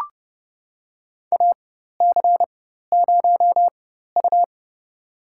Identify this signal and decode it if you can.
Morse code, heard as a keyed tone, about 32 wpm, spaced as for 14 wpm, 700 Hz. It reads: AC0U
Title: USA Call Signs - 4 Characters - 4x Character Spacing - 30wpm